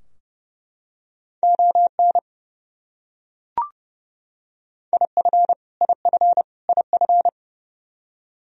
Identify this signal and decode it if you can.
Morse code, heard as a keyed tone, about 30 wpm, 700 Hz. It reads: ON IF IF IF
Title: Top 100 Words Repeated 30wpm